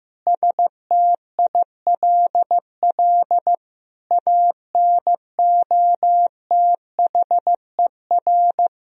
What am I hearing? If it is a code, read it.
Morse code, heard as a keyed tone, about 16 wpm, 700 Hz. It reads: STILL ANOTHER
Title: Sentences from Top 1000 Words - Review Part 2 - 15wpm